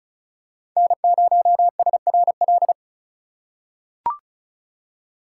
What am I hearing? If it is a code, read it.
Morse code, heard as a keyed tone, about 34 wpm, 700 Hz. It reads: N0SRL
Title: USA Call Signs - 5 Characters - 3x Character Spacing - 35wpm